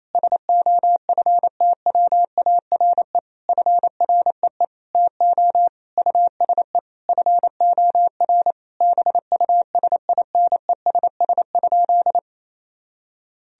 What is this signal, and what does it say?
Morse code, heard as a keyed tone, about 28 wpm, 700 Hz. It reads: SOFTWARE FREE TO USE FOR BUSINESS?
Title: Sentences from Top 1000 Words - Review Part 3 - 28wpm